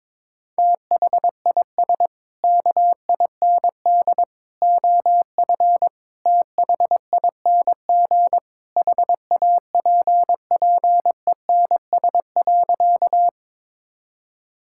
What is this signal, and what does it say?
Morse code, heard as a keyed tone, about 22 wpm, 700 Hz. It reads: THIS KIND OF THING HAPPENS.
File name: practice / 22wpm / Sentences with 3-Word Phrases - 22wpm